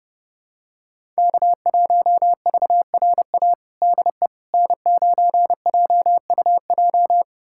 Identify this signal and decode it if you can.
Morse code, heard as a keyed tone, about 30 wpm, 700 Hz. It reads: K1VRA DE N9JUJ
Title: USA Callsign DE Callsign - 5 Characters - 30wpm